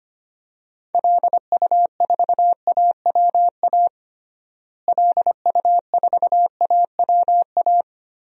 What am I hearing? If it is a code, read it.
Morse code, heard as a keyed tone, about 25 wpm, 700 Hz. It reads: LU4AWA LU4AWA
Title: DXCC-100 Call Signs - 6 Characters - Repeated 3x - 25wpm